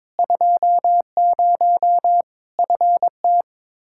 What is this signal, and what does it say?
Morse code, heard as a keyed tone, about 22 wpm, 700 Hz. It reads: 20 FT
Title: Practice QSO - Exchange 2 - Info - 22wpm